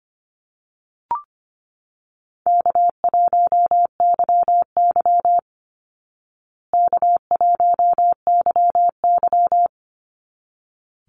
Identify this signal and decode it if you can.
Morse code, heard as a keyed tone, about 24 wpm, 700 Hz. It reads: K1YY K1YY
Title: Easy Call Signs Repeated 2x - 25wpm